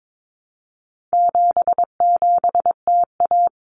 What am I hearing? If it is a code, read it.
Morse code, heard as a keyed tone, about 22 wpm, 700 Hz. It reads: 77TA
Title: Common Cut Numbers - 4-Digit 22wpm